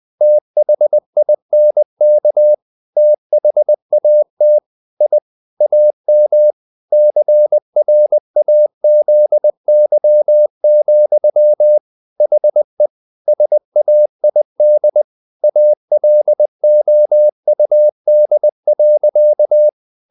Morse code, heard as a keyed tone, about 20 wpm, 600 Hz. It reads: THINK THAT I AM CRAZY, HE SAID ALOUD.
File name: the old man and the sea / the old man and the sea part 1 20wpm